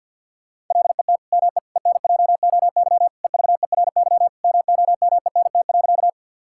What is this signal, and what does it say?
Morse code, heard as a keyed tone, about 46 wpm, 700 Hz. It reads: LET ME ENJOY EVERY MOMENT.
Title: Sentences from Top 1000 Words - Review Part 5 - 50wpm